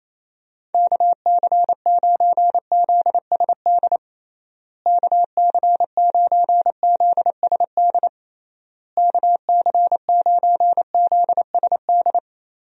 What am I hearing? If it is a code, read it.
Morse code, heard as a keyed tone, about 27 wpm, 700 Hz. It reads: KC9ZSD KC9ZSD KC9ZSD
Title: USA Call Signs - 6 Characters - Repeated 3x - 28wpm